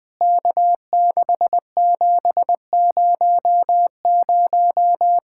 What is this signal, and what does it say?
Morse code, heard as a keyed tone, about 20 wpm, 700 Hz. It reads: K6700
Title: POTA QSO Practice - 20wpm